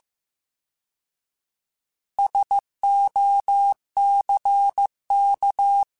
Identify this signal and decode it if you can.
Morse code, heard as a keyed tone, about 15 wpm, 800 Hz. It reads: SOCK
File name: Alphabetical / Alphabetical mp3s / Ss @13